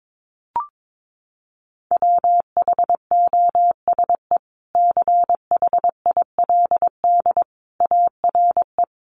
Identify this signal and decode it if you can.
Morse code, heard as a keyed tone, about 22 wpm, 700 Hz. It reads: WHOSE CHILD ARE
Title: Sentences from Top 1200 Words 22wpm